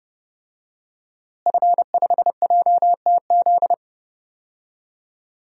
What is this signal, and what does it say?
Morse code, heard as a keyed tone, about 30 wpm, 700 Hz. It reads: F5JTZ
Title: DXCC-100 Call Signs - 5 Characters - Repeated 2x - 30wpm